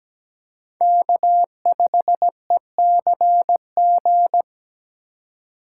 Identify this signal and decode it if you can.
Morse code, heard as a keyed tone, about 17 wpm, 700 Hz. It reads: K5ECG
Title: USA Call Signs - 5 Characters - 2x Character Spacing - 17wpm